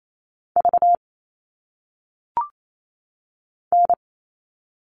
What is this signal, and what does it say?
Morse code, heard as a keyed tone, about 28 wpm, 700 Hz. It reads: V N
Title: All Cut Numbers - 1-Digit 28wpm